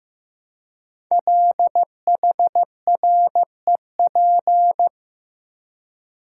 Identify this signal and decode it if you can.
Morse code, heard as a keyed tone, about 15 wpm, 700 Hz. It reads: LHREP
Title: Code Groups - 5 Letters - 15wpm